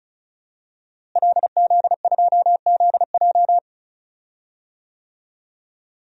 Morse code, heard as a keyed tone, about 36 wpm, 700 Hz. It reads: LZ2ZJ
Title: DXCC-100 Call Signs - 5 Characters - Repeated 2x - 35wpm